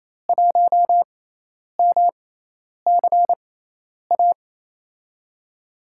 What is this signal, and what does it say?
Morse code, heard as a keyed tone, about 28 wpm, spaced as for 10 wpm, 700 Hz. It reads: 1MCA
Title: USA Call Signs - 6x Character Spacing - 28wpm